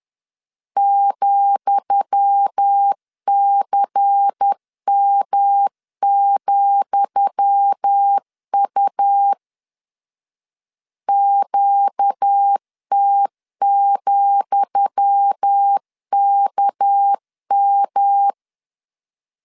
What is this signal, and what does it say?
Morse code, heard as a keyed tone, about 10 wpm, 800 Hz. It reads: ,CM,U QT,KM
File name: LEZIONI / 10 SIGNS / 10D COMA 10wpm 3013